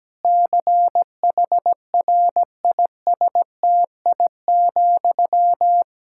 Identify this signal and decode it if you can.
Morse code, heard as a keyed tone, about 17 wpm, 700 Hz. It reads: CHRISTI,
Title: Practice QSO - Exchange 1 - Beginning - 17wpm